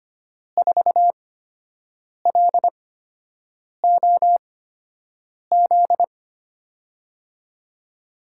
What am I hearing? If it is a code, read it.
Morse code, heard as a keyed tone, about 25 wpm, spaced as for 7 wpm, 700 Hz. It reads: 4LOZ
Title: USA Call Signs - 6 Characters - 8x Character Spacing - 25wpm